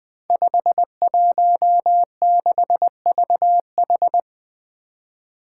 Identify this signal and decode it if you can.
Morse code, heard as a keyed tone, about 20 wpm, 700 Hz. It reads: 516VH
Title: Five Challenging Letter-Number Combinations 20wpm